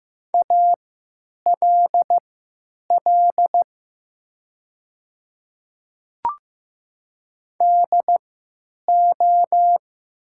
Morse code, heard as a keyed tone, about 15 wpm, spaced as for 8 wpm, 700 Hz. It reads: ALL DO
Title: 3x Character Spacing - Top 100 Words - 15wpm